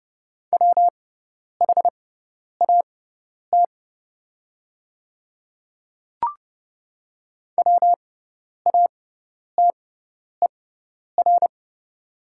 Morse code, heard as a keyed tone, about 30 wpm, spaced as for 10 wpm, 700 Hz. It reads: WHAT WATER
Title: 6x Character Spacing - Top 200 Words - 30wpm